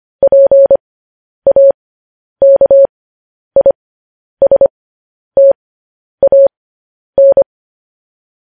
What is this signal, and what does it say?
Morse code, heard as a keyed tone, about 25 wpm, spaced as for 10 wpm, 550 Hz. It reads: PAKISTAN
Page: news headlines in morse code at 10 wpm